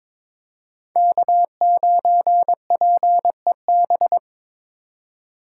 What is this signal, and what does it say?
Morse code, heard as a keyed tone, about 22 wpm, 700 Hz. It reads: K9PEB
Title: USA Call Signs - 5 Characters - 3x Character Spacing - 22wpm